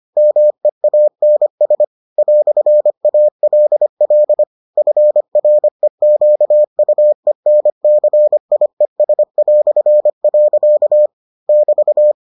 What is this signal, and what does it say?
Morse code, heard as a keyed tone, about 25 wpm, 600 Hz. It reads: MEANS "ALL FREQUENCIES". <BT>